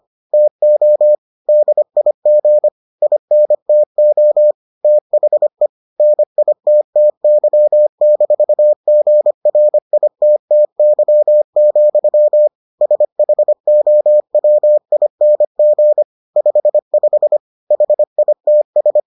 Morse code, heard as a keyed tone, about 25 wpm, 600 Hz. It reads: TO DIG INTO THE NITTY-GRITTY, SHOWING 55 HITS